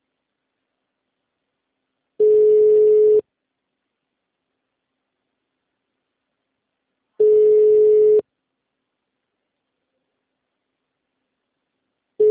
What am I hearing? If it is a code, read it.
Ringback tone (Europe, 425 Hz).
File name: out- -2024- - - .68